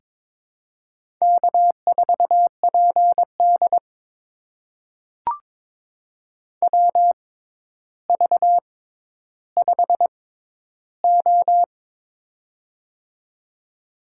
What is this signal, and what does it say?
Morse code, heard as a keyed tone, about 22 wpm, spaced as for 11 wpm, 700 Hz. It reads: K4PD WV5O
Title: USA Call Signs - 4 Characters - 6x Character Spacing - 22wpm